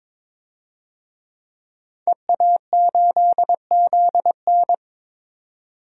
Morse code, heard as a keyed tone, about 22 wpm, 700 Hz. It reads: EA8ZN
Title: DXCC-100 Call Signs - 5 Characters - 22wpm